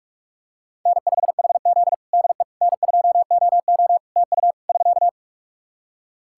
Morse code, heard as a keyed tone, about 46 wpm, 700 Hz. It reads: N5HZ DE N2OO TU <SK>
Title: Practice QSO - Exchange 4 - Ending - 45wpm